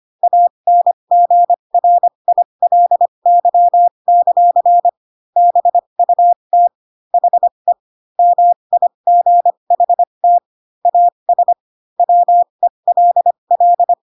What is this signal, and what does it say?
Morse code, heard as a keyed tone, about 25 wpm, spaced as for 22 wpm, 700 Hz. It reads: ANGRILY; BUT HE MIGHT AS WELL
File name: Pinocchio / Pinocchio ch3 18 25 ik1hge